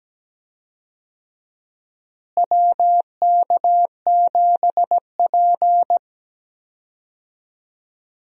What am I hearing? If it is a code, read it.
Morse code, heard as a keyed tone, about 17 wpm, 700 Hz. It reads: WK7P
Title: USA Call Signs - 4 Characters - Repeated 2x - 17wpm